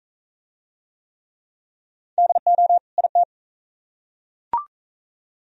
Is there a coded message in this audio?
Yes: Morse code from 2.2 to 3.2 s, heard as a keyed tone, about 41 wpm, 700 Hz. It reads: DO IT